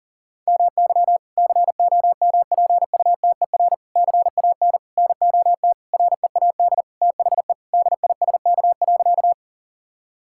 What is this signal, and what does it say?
Morse code, heard as a keyed tone, about 40 wpm, 700 Hz. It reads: MY COMPUTER CAN NOT READ THE DISK.
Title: Sentences from Top 1000 Words - Review Part 6 - 40wpm